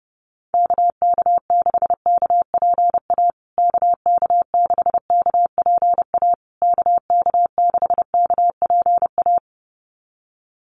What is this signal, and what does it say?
Morse code, heard as a keyed tone, about 30 wpm, 700 Hz. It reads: KK6KPA KK6KPA KK6KPA
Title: USA Call Signs Repeated 3x 30wpm